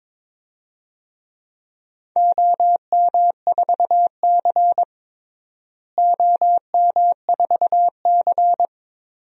Morse code, heard as a keyed tone, about 21 wpm, 700 Hz. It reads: OM4C OM4C
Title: DXCC-100 Call Signs - 4 Characters - Repeated 3x - 22wpm